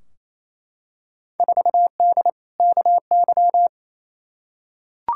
Morse code, heard as a keyed tone, about 28 wpm, 700 Hz. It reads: 4D KY
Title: Field Day Exchanges Encore - 28wpm